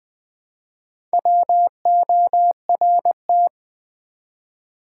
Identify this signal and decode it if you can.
Morse code, heard as a keyed tone, about 20 wpm, 700 Hz. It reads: WORT
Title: Build A Word - 1 Letter At A Time - 5-Letter Words - 20wpm